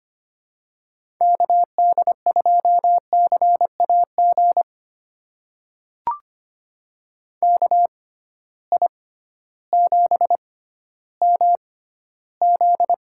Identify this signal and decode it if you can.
Morse code, heard as a keyed tone, about 25 wpm, spaced as for 13 wpm, 700 Hz. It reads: KD2CAG KI7MZ
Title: USA Call Signs - 6 Characters - 6x Character Spacing - 25wpm